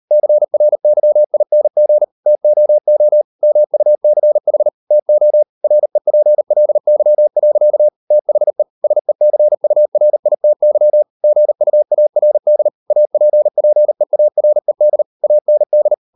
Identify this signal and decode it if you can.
Morse code, heard as a keyed tone, about 38 wpm, 600 Hz. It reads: CRYING TOO MUCH TO REPLY. THE SECURITY GUARD APPEARED AND